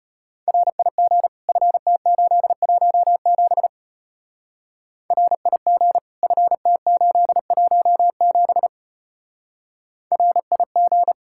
Morse code, heard as a keyed tone, about 30 wpm, 700 Hz. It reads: RIG FT817 RIG FT817 RIG
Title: Speed-Racing Common QSO Phrases - 25wpm